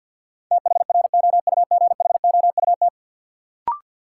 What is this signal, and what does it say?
Morse code, heard as a keyed tone, about 53 wpm, 700 Hz. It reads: THROUGHOUT
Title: Sentences from Top 800 Words - New Word Drill - 50wpm